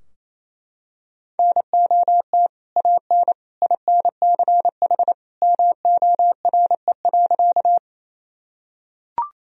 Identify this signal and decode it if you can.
Morse code, heard as a keyed tone, about 28 wpm, 700 Hz. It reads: NOT AN INCH MORE.
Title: Sentences from Top 1000 Words - Review Part 3 - 28wpm